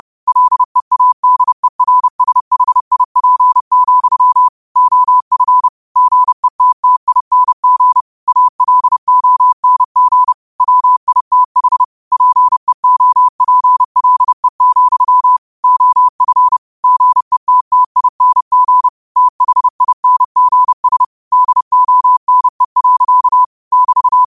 Morse code, heard as a keyed tone, about 30 wpm, 1000 Hz. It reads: LEADERSHIP, OF GETTING ALONG WITH PEOPLE, OF GETTING THINGS DONE. <BT>